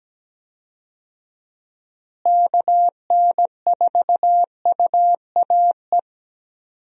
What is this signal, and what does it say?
Morse code, heard as a keyed tone, about 17 wpm, 700 Hz. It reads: KN4UAE
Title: USA Call Signs - 6 Characters - Repeated 3x - 17wpm